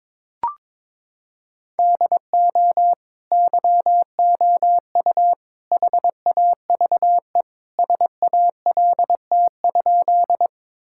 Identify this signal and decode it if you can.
Morse code, heard as a keyed tone, about 22 wpm, 700 Hz. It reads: DO YOU HAVE SALT?
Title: Sentences from Top 1000 Words - Review Part 5 - 22wpm